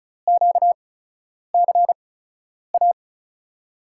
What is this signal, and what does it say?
Morse code, heard as a keyed tone, about 34 wpm, spaced as for 10 wpm, 700 Hz. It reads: QCA
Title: USA Call Signs - 6 Characters - 8x Character Spacing - 35wpm